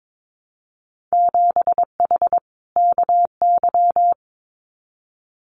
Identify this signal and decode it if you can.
Morse code, heard as a keyed tone, about 22 wpm, 700 Hz. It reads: 7H KY
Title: Winter Field Day Exchanges Encore 22wpm